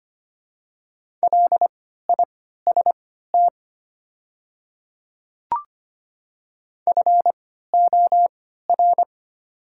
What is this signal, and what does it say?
Morse code, heard as a keyed tone, about 25 wpm, spaced as for 14 wpm, 690 Hz. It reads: LIST FOR
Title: 3x Character Spacing - Top 500 Words - 25wpm